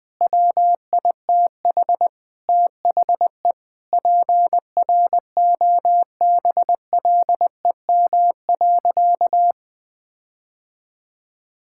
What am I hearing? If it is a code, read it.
Morse code, heard as a keyed tone, about 20 wpm, 700 Hz. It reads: WITH THE PROBLEM.
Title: Sentences with 5-Word Phrases - 20wpm